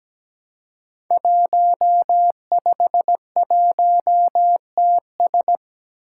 Morse code, heard as a keyed tone, about 17 wpm, 700 Hz. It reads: 151TS
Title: Five Challenging Letter-Number Combinations 17wpm